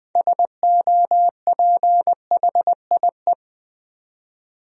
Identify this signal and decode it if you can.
Morse code, heard as a keyed tone, about 20 wpm, 700 Hz. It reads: SOPHIE
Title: USA Common Names - 20wpm